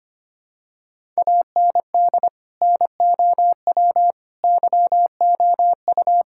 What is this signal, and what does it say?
Morse code, heard as a keyed tone, about 25 wpm, 700 Hz. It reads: AND NOW YOU
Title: Sentences from Top 1000 Words - Review Part 6 - 25wpm